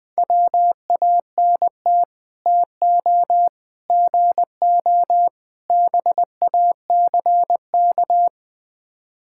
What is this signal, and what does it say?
Morse code, heard as a keyed tone, about 20 wpm, 700 Hz. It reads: WANT TO GO BACK
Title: Build A Phrase - 5-Word Phrases - 20wpm